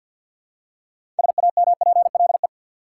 Morse code, heard as a keyed tone, about 47 wpm, 700 Hz. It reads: SAMPLE
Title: Six Letter Words - 50wpm